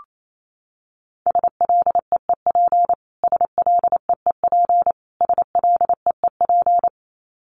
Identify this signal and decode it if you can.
Morse code, heard as a keyed tone, about 28 wpm, 700 Hz. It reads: SLEEP SLEEP SLEEP
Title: Top 500 Words Repeated 28wpm